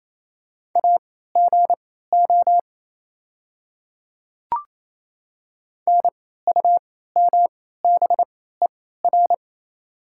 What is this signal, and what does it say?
Morse code, heard as a keyed tone, about 28 wpm, spaced as for 16 wpm, 700 Hz. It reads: AGO NUMBER
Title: 3x Character Spacing - Top 500 Words - 28wpm